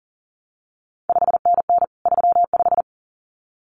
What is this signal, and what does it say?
Morse code, heard as a keyed tone, about 38 wpm, 700 Hz. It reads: 5NN 35
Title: CQ WW DX Contest Exchange Encore - 40wpm